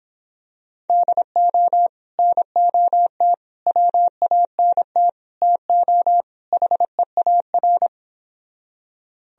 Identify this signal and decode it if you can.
Morse code, heard as a keyed tone, about 26 wpm, 700 Hz. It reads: DO NOT WANT TO HEAR